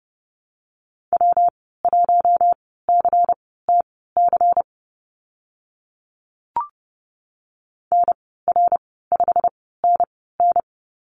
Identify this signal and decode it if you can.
Morse code, heard as a keyed tone, about 30 wpm, spaced as for 17 wpm, 700 Hz. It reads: W1CTC NR5NN
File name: Lesson 19g - Letter w - Call Signs 30wpm